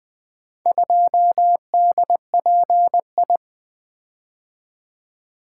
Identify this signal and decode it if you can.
Morse code, heard as a keyed tone, about 20 wpm, 700 Hz. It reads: 2DPI
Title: Four Letter-Number Combinations - 20wpm